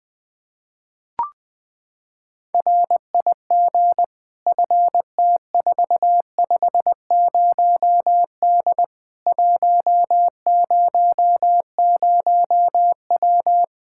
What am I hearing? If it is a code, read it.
Morse code, heard as a keyed tone, about 20 wpm, 700 Hz. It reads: RIG FT450D 100W
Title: Practice QSO - Exchange 2 - Info - 20wpm